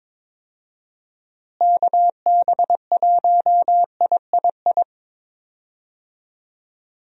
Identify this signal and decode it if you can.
Morse code, heard as a keyed tone, about 22 wpm, 700 Hz. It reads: KB1III
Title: USA Call Signs - 6 Characters - Repeated 3x - 22wpm